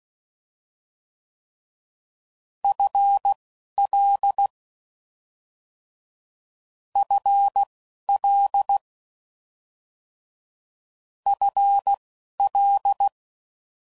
Morse code, heard as a keyed tone, about 16 wpm, spaced as for 11 wpm, 800 Hz. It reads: FL FL FL
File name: Mixed / Short Runs / ShortContrasts